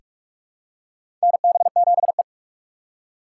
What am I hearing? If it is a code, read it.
Morse code, heard as a keyed tone, about 46 wpm, 700 Hz. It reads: NB7E
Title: USA Call Signs - 4 Characters - 4x Character Spacing - 45wpm